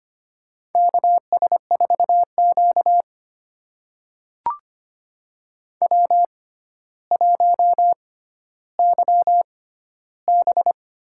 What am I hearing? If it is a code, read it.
Morse code, heard as a keyed tone, about 25 wpm, spaced as for 13 wpm, 700 Hz. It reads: KS4Q W1YB
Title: USA Call Signs - 4 Characters - 6x Character Spacing - 25wpm